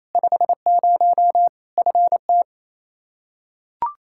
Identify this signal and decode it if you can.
Morse code, heard as a keyed tone, about 27 wpm, 700 Hz. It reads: 50 FT
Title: Practice QSO - Exchange 2 - Info - 28wpm